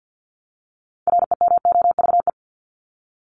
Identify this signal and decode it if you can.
Morse code, heard as a keyed tone, about 48 wpm, 700 Hz. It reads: REMOVE
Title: Top 751-1000 Words 50wpm